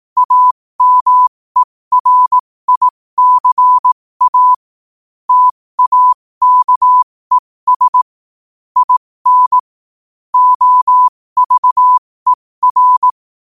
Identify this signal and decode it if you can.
Morse code, heard as a keyed tone, about 18 wpm, spaced as for 15 wpm, 1000 Hz. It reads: AMERICA TAKES IN OVER
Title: 15 WPM morse code quotes for Fri, 01 Mar 2024